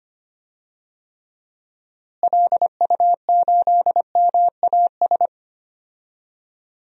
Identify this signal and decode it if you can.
Morse code, heard as a keyed tone, about 25 wpm, 700 Hz. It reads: LU8MAS
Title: DXCC-100 Call Signs - 6 Characters - Repeated 2x - 25wpm